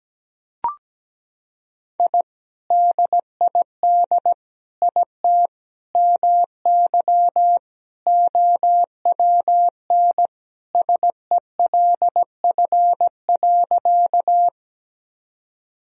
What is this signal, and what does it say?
Morse code, heard as a keyed tone, about 17 wpm, 700 Hz. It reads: I DID IT MY OWN SELF.
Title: Sentences from Top 1000 Words - Review Part 1 - 17wpm